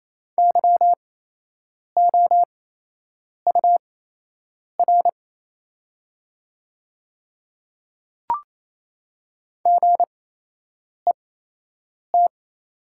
Morse code, heard as a keyed tone, about 28 wpm, spaced as for 8 wpm, 700 Hz. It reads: YOUR GET
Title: 8x Character Spacing - Top 200 Words - 28wpm